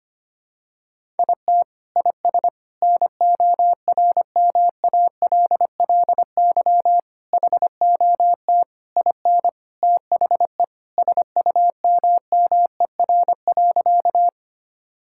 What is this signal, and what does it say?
Morse code, heard as a keyed tone, about 25 wpm, 700 Hz. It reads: IT IS NORMALLY HOT IN THE SUMMER.
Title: Sentences from Top 1000 Words - Review Part 6 - 25wpm